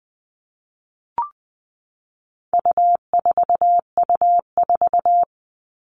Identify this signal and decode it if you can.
Morse code, heard as a keyed tone, about 20 wpm, 700 Hz. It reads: U4U4
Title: All Cut Numbers - 4-Digit 20wpm